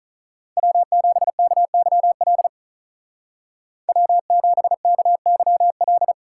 Morse code, heard as a keyed tone, about 41 wpm, 700 Hz. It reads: W7KYL W7KYL
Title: Speed Racing - USA 1x3 Call Signs 35wpm